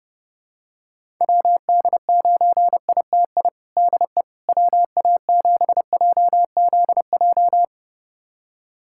Morse code, heard as a keyed tone, about 32 wpm, 700 Hz. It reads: WD9ITI DE WA7JZJ
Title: USA Callsign DE Callsign - 6 Characters - 30wpm